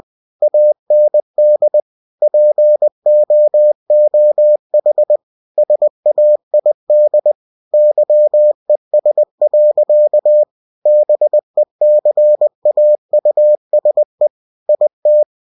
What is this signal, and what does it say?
Morse code, heard as a keyed tone, about 20 wpm, 600 Hz. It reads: AND POOH SAID YES. BECAUSE IT
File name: cw-books / winnie-the-pooh / winnie-20wpm-page-0038